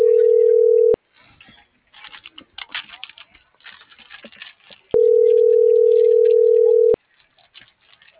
Ringback tone (North America), 440 + 480 Hz.
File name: CENTRAL MULTIMODAL / out angel.velazquez